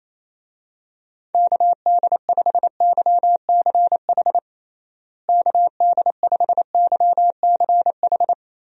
Morse code, heard as a keyed tone, about 27 wpm, 700 Hz. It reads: KD5YCH KD5YCH
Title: USA Call Signs - 6 Characters - Repeated 3x - 28wpm